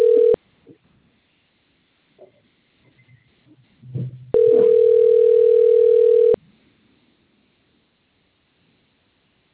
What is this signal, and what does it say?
Ringback tone (North America), 440 + 480 Hz.